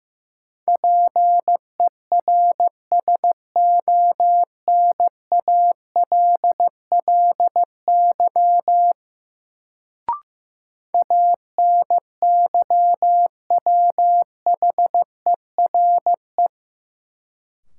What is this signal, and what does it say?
Morse code, heard as a keyed tone, about 15 wpm, 700 Hz. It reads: PERSONALLY ANYWHERE
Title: Sentences from Top 1400 Words - New Word Drill Alt - 15wpm